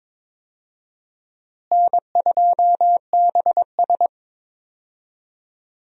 Morse code, heard as a keyed tone, about 22 wpm, 700 Hz. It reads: N2BS
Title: USA Call Signs - 4 Characters - Repeated 3x - 22wpm